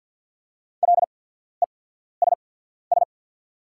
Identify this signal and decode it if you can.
Morse code, heard as a keyed tone, about 48 wpm, spaced as for 14 wpm, 700 Hz. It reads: LESS
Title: 8x Character Spacing - Top 500 Words - 50wpm